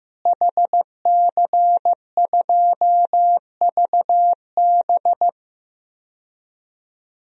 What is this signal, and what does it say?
Morse code, heard as a keyed tone, about 15 wpm, 700 Hz. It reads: HC2VB
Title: DXCC-100 Call Signs - 5 Characters - Repeated 2x - 15wpm